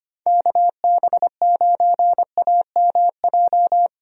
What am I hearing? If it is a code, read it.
Morse code, heard as a keyed tone, about 25 wpm, 700 Hz. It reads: KB9AMJ
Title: USA Call Signs - 6 Characters - Repeated 3x - 25wpm